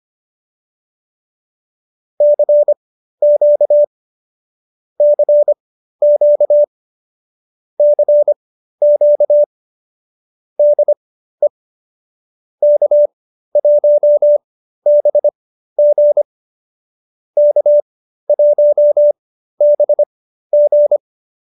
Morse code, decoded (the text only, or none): CQ CQ CQ DE K1BG K1BG